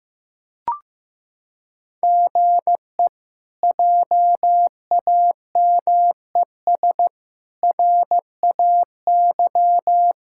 Morse code, heard as a keyed tone, about 15 wpm, 700 Hz. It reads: GE JAMES RAY
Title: K1USN SST QSO Practice - 15wpm